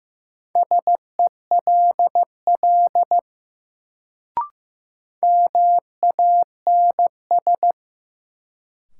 Morse code, heard as a keyed tone, about 15 wpm, 700 Hz. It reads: SELL MANS
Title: Sentences from Top 1000 Words - Review Part 6 - New Word Drill - 15wpm